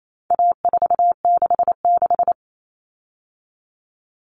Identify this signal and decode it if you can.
Morse code, heard as a keyed tone, about 28 wpm, 700 Hz. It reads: A466
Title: Common Cut Numbers - 4-Digit 28wpm